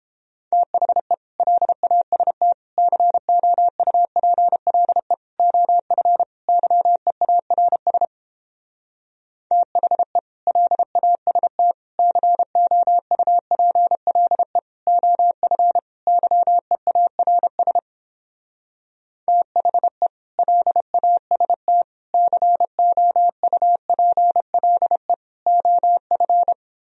Morse code, heard as a keyed tone, about 29 wpm, 700 Hz. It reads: THE LAST COUPLE OF YEARS THE LAST COUPLE OF YEARS THE LAST COUPLE OF